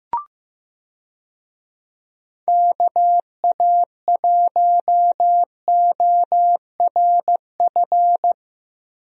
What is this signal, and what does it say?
Morse code, heard as a keyed tone, about 16 wpm, 700 Hz. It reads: KA1ORF
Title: USA Call Signs - 6 Characters - Repeated 3x - 15wpm